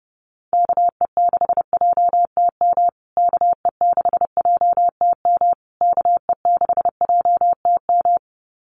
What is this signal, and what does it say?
Morse code, heard as a keyed tone, about 29 wpm, 700 Hz. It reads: KE6JTM KE6JTM KE6JTM
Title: USA Call Signs Repeated 3x 30wpm